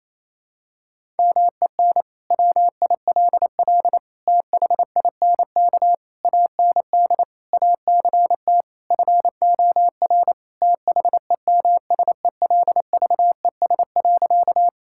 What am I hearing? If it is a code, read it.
Morse code, heard as a keyed tone, about 28 wpm, 700 Hz. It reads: MEN WILL THINK AND ACT FOR THEMSELVES.
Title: Sentences from Top 1000 Words - Review Part 5 - 28wpm